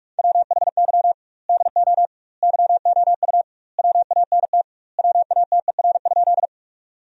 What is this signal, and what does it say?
Morse code, heard as a keyed tone, about 43 wpm, 700 Hz. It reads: WHY DO YOU WANT WATER?
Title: 2x Word-Spacing - Sentences from Top 200 Words - 45wpm